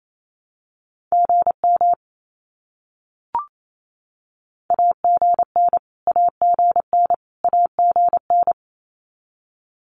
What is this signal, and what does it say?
Morse code, heard as a keyed tone, about 28 wpm, 700 Hz. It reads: GM AGN AGN AGN
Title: Abbreviations Repeated 3x - 28wpm